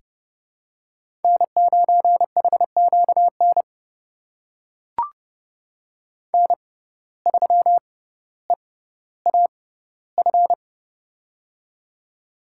Morse code, heard as a keyed tone, about 30 wpm, spaced as for 15 wpm, 700 Hz. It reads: N9HQN N3EAF
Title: USA Call Signs - 6x Character Spacing - 30wpm